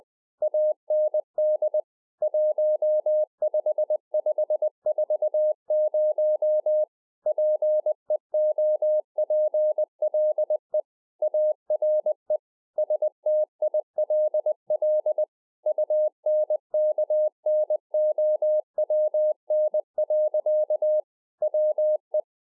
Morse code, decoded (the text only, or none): AND 15540 PEOPLE ARE STILL UNKNOWN. WE